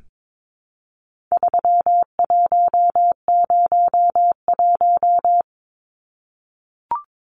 Morse code, heard as a keyed tone, about 22 wpm, 700 Hz. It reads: 3101
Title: Numbers - 4-Digit 22wpm